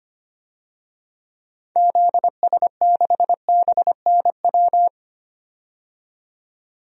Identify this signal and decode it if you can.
Morse code, heard as a keyed tone, about 25 wpm, 700 Hz. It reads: ZS6BNW
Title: DXCC-100 Call Signs - 6 Characters - 25wpm